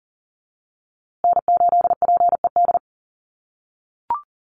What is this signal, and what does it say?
Morse code, heard as a keyed tone, about 40 wpm, 700 Hz. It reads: N8PED
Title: USA Call Signs Repeated 2x 40wpm